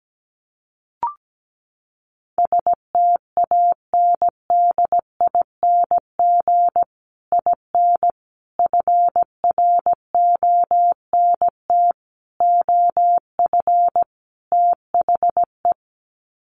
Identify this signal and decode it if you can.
Morse code, heard as a keyed tone, about 17 wpm, 700 Hz. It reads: STANDING IN FRONT OF THE
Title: Sets of 5 Words Encore2 17wpm